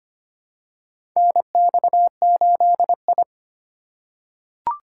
Morse code, heard as a keyed tone, about 25 wpm, 700 Hz. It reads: NX8I
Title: USA Call Signs - 4 Characters - 3x Character Spacing - 25wpm